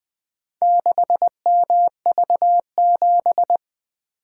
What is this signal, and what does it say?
Morse code, heard as a keyed tone, about 20 wpm, 700 Hz. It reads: 6MV7
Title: Four Letter-Number Combinations - 20wpm